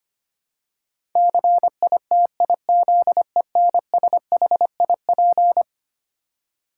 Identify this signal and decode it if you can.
Morse code, heard as a keyed tone, about 25 wpm, 700 Hz. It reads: CITIZENSHIP
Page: morse-code-ninja-practice-25wpm